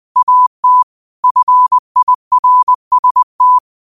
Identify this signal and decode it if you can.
Morse code, heard as a keyed tone, about 20 wpm, 1000 Hz. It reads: AT FIRST